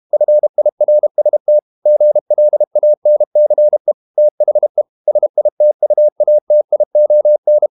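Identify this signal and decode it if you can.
Morse code, heard as a keyed tone, about 32 wpm, 600 Hz. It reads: FIRST GLANCE THE SITUATION
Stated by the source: Morse code practice at 32 wpm from CW med Gnister, reading Ken Follett's novel Never in English.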